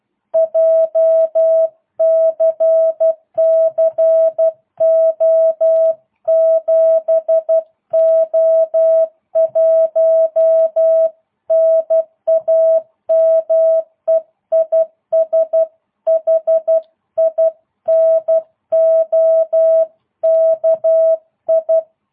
Morse code, heard as a keyed tone, about 12 wpm, 660 Hz. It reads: JCC O7O1 NAME IS HINOKI